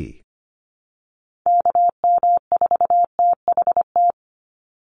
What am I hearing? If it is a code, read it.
Morse code, heard as a keyed tone, about 25 wpm, 700 Hz. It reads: KM4THT
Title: USA Call Signs 25wpm